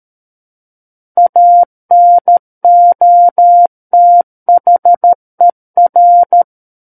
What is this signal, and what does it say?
Morse code, heard as a keyed tone, about 13 wpm, 700 Hz. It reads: ANOTHER
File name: Book One / 04-Chapter 4 - 13 WPM